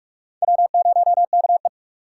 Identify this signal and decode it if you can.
Morse code, heard as a keyed tone, about 60 wpm, spaced as for 52 wpm, 700 Hz. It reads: W0KE
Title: Speed Racing - USA 1x2 Call Signs 45wpm